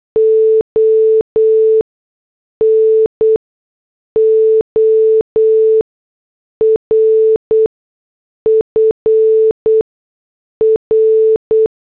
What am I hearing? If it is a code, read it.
Morse code, heard as a keyed tone, about 8 wpm, spaced as for 6 wpm, 440 Hz. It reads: ON OR FR